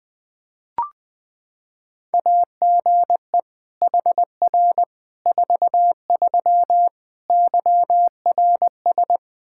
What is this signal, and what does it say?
Morse code, heard as a keyed tone, about 20 wpm, 700 Hz. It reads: AGE HR 43 YRS
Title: K1USN SST QSO Practice 20wpm